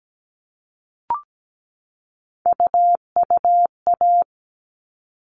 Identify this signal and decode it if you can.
Morse code, heard as a keyed tone, about 17 wpm, 700 Hz. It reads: UUA
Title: All Cut Numbers - 3-Digit 17wpm